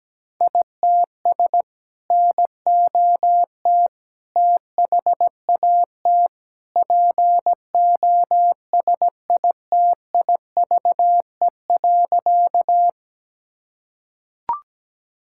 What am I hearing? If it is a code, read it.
Morse code, heard as a keyed tone, about 17 wpm, 700 Hz. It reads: ITS NOT THAT POSITIVE.
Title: Sentences from Top 1000 Words - Review Part 5 - 17wpm